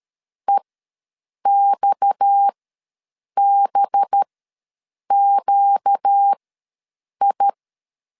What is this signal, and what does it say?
Morse code, heard as a keyed tone, about 13 wpm, spaced as for 7 wpm, 800 Hz. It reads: EXBQI
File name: LEZIONI / 6 PKX0 / 6 PKX ws 12wpm 3106